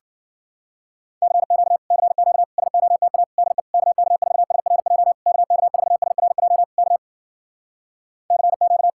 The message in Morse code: CQ CQ SOTA DE KK4SRJ KK4SRJ K CQ

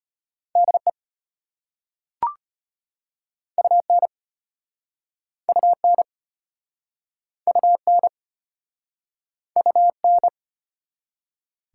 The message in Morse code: DE UN UN UN UN

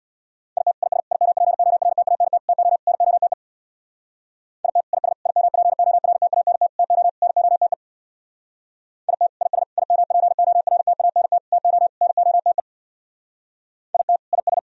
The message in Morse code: IT IS IMPORTANT TO NOTE IT IS IMPORTANT TO NOTE IT IS IMPORTANT TO NOTE IT IS

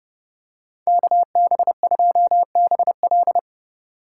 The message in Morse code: KB2BL

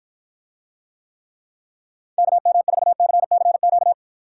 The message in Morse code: KM4CCQ

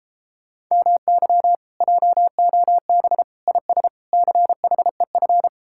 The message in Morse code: MY JOB IS CHEF

